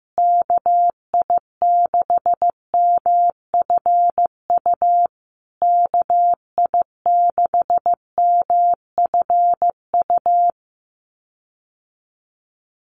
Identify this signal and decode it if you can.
Morse code, heard as a keyed tone, about 15 wpm, 700 Hz. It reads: KI6MFU KI6MFU
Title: USA Call Signs Repeated 3x 15wpm